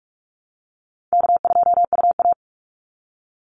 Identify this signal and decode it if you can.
Morse code, heard as a keyed tone, about 44 wpm, 700 Hz. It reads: K2UA
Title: Easy Call Signs - 45wpm